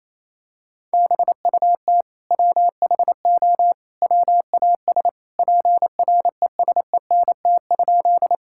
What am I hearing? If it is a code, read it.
Morse code, heard as a keyed tone, about 28 wpm, 700 Hz. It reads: BUT WHO WAS PRESENT?
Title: Sentences from Top 1000 Words - Review Part 5 - 28wpm